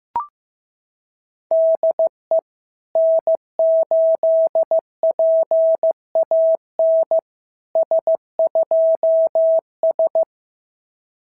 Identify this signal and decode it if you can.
Morse code, heard as a keyed tone, about 14 wpm, 650 Hz. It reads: DE N8PAN S2S
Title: SOTA QSO Practice 15wpm